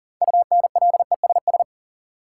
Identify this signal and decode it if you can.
Morse code, heard as a keyed tone, about 38 wpm, 700 Hz. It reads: UNLESS
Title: Sentences from Top 1000 Words - Review Part 5 - New Word Drill - 40wpm